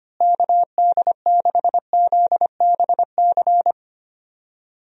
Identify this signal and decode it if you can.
Morse code, heard as a keyed tone, about 25 wpm, 700 Hz. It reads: KD6ZBC
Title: USA Call Signs - 6 Characters - Repeated 3x - 25wpm